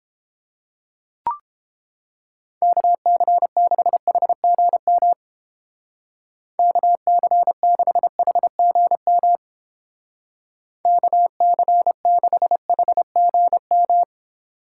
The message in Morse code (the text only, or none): KC6HGM KC6HGM KC6HGM